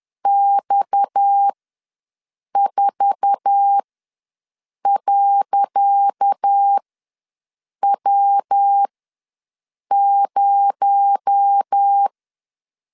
Morse code, heard as a keyed tone, about 10 wpm, spaced as for 6 wpm, 800 Hz. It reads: X4.W0
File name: LEZIONI / 10 SIGNS / 10C DOT ws 10wpm 4213